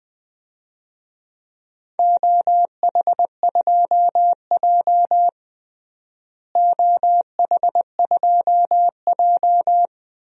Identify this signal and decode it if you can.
Morse code, heard as a keyed tone, about 21 wpm, 700 Hz. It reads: OH2J OH2J
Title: DXCC-100 Call Signs - 4 Characters - Repeated 2x - 20wpm